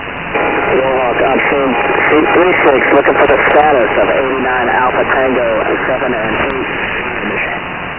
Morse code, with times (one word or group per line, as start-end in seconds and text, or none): none